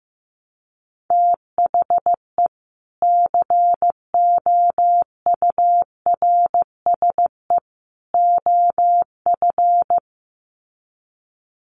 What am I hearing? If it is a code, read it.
Morse code, heard as a keyed tone, about 15 wpm, 700 Hz. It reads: THE COURSE OF
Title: Sets of 3 Words 15wpm